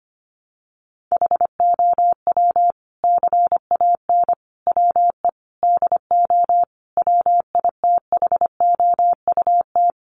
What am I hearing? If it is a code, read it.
Morse code, heard as a keyed tone, about 25 wpm, 700 Hz. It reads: HOW CAN WE DO WITHOUT
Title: Sentences from Top 600 Words 25wpm